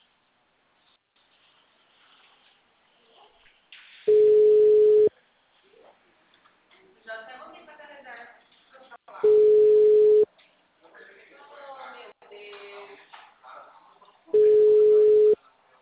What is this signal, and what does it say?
Ringback tone (Europe, 425 Hz), with speech.
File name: out- -2024- - - .72423